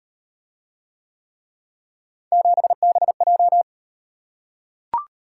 Morse code, heard as a keyed tone, about 36 wpm, 700 Hz. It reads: 7BJ